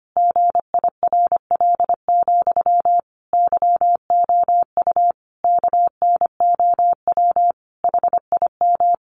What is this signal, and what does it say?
Morse code, heard as a keyed tone, about 25 wpm, 700 Hz. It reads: GIRL, YOU KNOW HIM
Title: Sentences from Top 1000 Words 25wpm